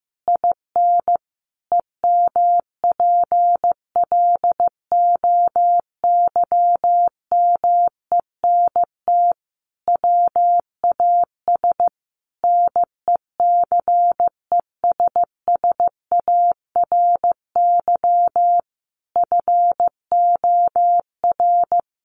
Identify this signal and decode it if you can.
Morse code, heard as a keyed tone, about 14 wpm, 700 Hz. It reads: IN EMPLOYMENT WAS NECESSARY FOR
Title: Sentences from Top 1500 Words 15wpm